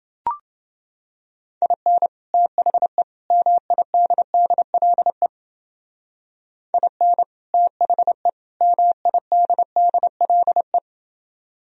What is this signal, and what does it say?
Morse code, heard as a keyed tone, about 29 wpm, 700 Hz. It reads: IN THE MIDDLE IN THE MIDDLE